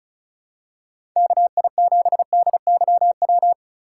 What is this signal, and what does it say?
Morse code, heard as a keyed tone, about 36 wpm, 700 Hz. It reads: KI7DYW
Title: USA Call Signs - 6 Characters - 2x Character Spacing - 35wpm